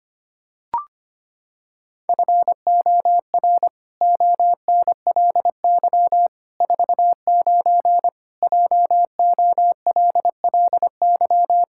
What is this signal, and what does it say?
Morse code, heard as a keyed tone, about 25 wpm, 700 Hz. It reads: FOR ONLY 49 JOLLY
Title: Pangram Challenge v2 - 25wpm